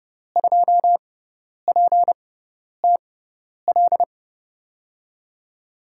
Morse code, heard as a keyed tone, about 30 wpm, spaced as for 10 wpm, 700 Hz. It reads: 2PTL
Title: USA Call Signs - 5 Characters - 6x Character Spacing - 30wpm